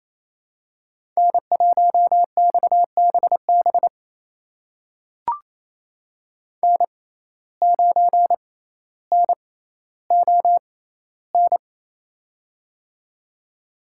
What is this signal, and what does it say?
Morse code, heard as a keyed tone, about 28 wpm, spaced as for 14 wpm, 700 Hz. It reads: N1XBB N9NON
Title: USA Call Signs - 5 Characters - 6x Character Spacing - 28wpm